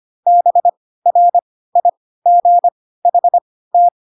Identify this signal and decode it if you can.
Morse code, heard as a keyed tone, about 25 wpm, spaced as for 16 wpm, 700 Hz. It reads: BRIGHT
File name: Pinocchio / Pinocchio ch6 10 25 ik1hge